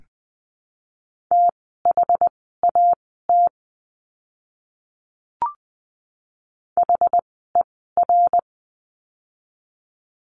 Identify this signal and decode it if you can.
Morse code, heard as a keyed tone, about 20 wpm, spaced as for 15 wpm, 700 Hz. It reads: THAT HER
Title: 160 Lesson 26f - Letter g - Words 20wpm